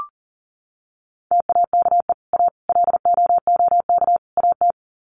Morse code, heard as a keyed tone, about 40 wpm, 700 Hz. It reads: TAKE A LOOK AT
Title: Sets of 4 Words 40wpm